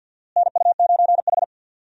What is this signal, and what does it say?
Morse code, heard as a keyed tone, about 47 wpm, 700 Hz. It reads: NU9H
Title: USA Call Signs - 4 Characters - Repeated 3x - 50wpm